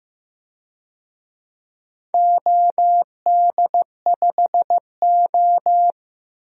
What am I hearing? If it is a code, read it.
Morse code, heard as a keyed tone, about 15 wpm, 700 Hz. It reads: OD5O
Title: DXCC-100 Call Signs - 4 Characters - 15wpm